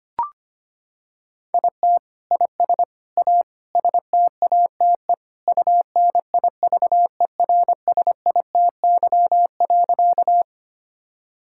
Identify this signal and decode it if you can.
Morse code, heard as a keyed tone, about 25 wpm, 700 Hz. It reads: IT IS A STATE UNIVERSITY.
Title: Sentences from Top 1000 Words - Review Part 5 - 25wpm